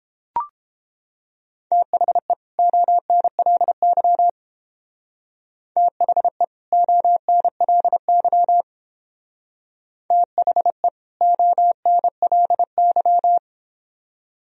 Morse code, heard as a keyed tone, about 30 wpm, 700 Hz. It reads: THE ONLY THE ONLY THE ONLY